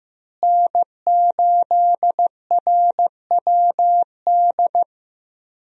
Morse code, heard as a keyed tone, about 16 wpm, 700 Hz. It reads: N8RWD
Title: USA Call Signs - 5 Characters - Repeated 2x - 15wpm